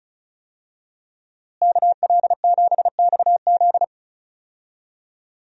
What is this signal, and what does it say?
Morse code, heard as a keyed tone, about 34 wpm, 700 Hz. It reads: KL7XZ
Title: DXCC-100 Call Signs - 5 Characters - Repeated 3x - 35wpm